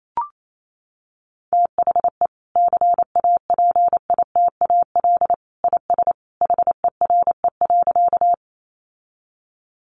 Morse code, heard as a keyed tone, about 28 wpm, 700 Hz. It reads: THE CAPITAL IS HERE.
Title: Sentences from Top 1000 Words 28wpm